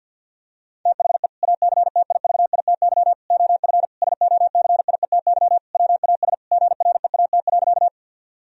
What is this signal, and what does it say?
Morse code, heard as a keyed tone, about 50 wpm, 700 Hz. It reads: THE ACTIVITY OF SOCIETY WAS GREAT.
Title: Sentences from Top 1000 Words - Review Part 4 - 50wpm